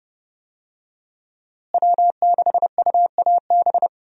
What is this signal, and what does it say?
Morse code, heard as a keyed tone, about 30 wpm, 700 Hz. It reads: W6UAB
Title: USA Call Signs - 5 Characters - Repeated 2x - 30wpm